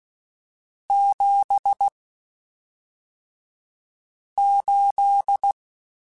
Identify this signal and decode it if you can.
Morse code, heard as a keyed tone, about 16 wpm, spaced as for 3 wpm, 800 Hz. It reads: 78
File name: (spaced) / QSOs Spaced-mp3s / WA7QOX 2 @5 sp